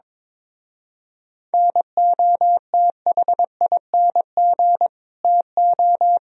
Morse code, heard as a keyed tone, about 23 wpm, 700 Hz. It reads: NOTHING TO
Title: Speed-Racing - Sets of 5 Words 22wpm